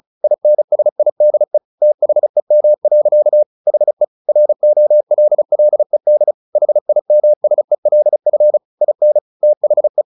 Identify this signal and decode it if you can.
Morse code, heard as a keyed tone, about 35 wpm, 600 Hz. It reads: INSIDE THEM. HE ROLLED HIMSELF IN THE
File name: the old man and the sea / the old man and the sea 1 35wpm